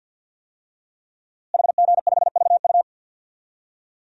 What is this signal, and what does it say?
Morse code, heard as a keyed tone, about 51 wpm, 700 Hz. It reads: HG5VU
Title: DXCC-100 Call Signs - 5 Characters - Repeated 3x - 50wpm